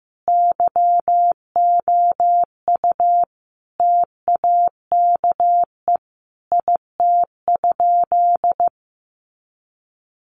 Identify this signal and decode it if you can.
Morse code, heard as a keyed tone, about 15 wpm, 700 Hz. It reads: YOU TAKE IT?
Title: Sentences from Top 200 Words 15wpm